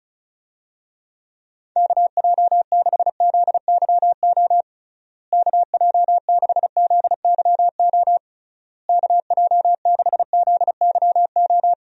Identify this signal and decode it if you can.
Morse code, heard as a keyed tone, about 36 wpm, 700 Hz. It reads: KJ6ZYO KJ6ZYO KJ6ZYO
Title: USA Call Signs - 6 Characters - Repeated 3x - 35wpm